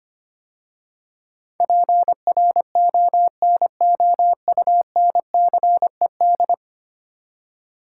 Morse code, heard as a keyed tone, about 25 wpm, 700 Hz. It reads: PRONOUNCED
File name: 10 Letter Words - First 3 Letters Given - 25wpm